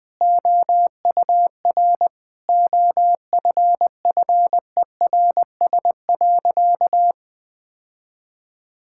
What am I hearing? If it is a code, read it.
Morse code, heard as a keyed tone, about 19 wpm, 700 Hz. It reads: OUR OFFERS.
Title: Sentences with 6-Word Phrases - 20wpm